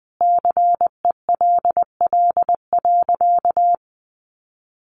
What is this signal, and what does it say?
Morse code, heard as a keyed tone, about 20 wpm, 700 Hz. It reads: CELL.
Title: Sentences from Top 1100 Words 20wpm